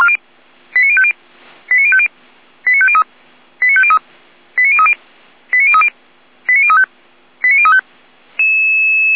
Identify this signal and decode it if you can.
Five-tone selective call (ZVEI), heard as quick runs of five tones, tones of about 70 ms. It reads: 89050 89050 89503 89503 89030 89030 89035 89035